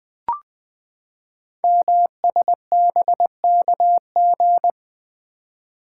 Morse code, heard as a keyed tone, about 20 wpm, 700 Hz. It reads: MSBKG
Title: Code Groups - 5 Letters - 20wpm